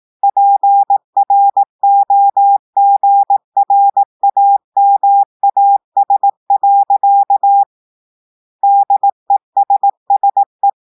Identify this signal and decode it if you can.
Morse code, heard as a keyed tone, about 18 wpm, 800 Hz. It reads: PROGRAMAS. DESSE
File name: audios / portugais / portugais texte en clair 2 vitesse 18 mots minutes